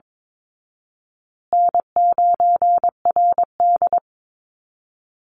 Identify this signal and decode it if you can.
Morse code, heard as a keyed tone, about 22 wpm, 700 Hz. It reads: N9RD
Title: Call Signs Repeated 2x - 22wpm